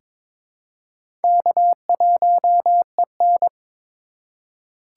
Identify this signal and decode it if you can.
Morse code, heard as a keyed tone, about 22 wpm, 700 Hz. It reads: K1EN
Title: USA Call Signs - 4 Characters - 2x Character Spacing - 22wpm